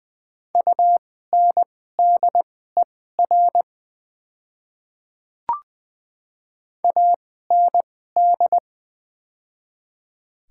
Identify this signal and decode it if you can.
Morse code, heard as a keyed tone, about 20 wpm, spaced as for 14 wpm, 700 Hz. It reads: UNDER AND
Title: 2x Character Spacing - Top 300 Words - 20wpm